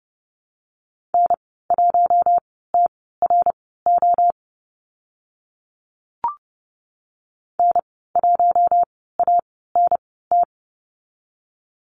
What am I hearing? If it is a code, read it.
Morse code, heard as a keyed tone, about 29 wpm, spaced as for 17 wpm, 700 Hz. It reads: N1TRO N1ANT
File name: Lesson 14g - Number 2 - Call Signs 30wpm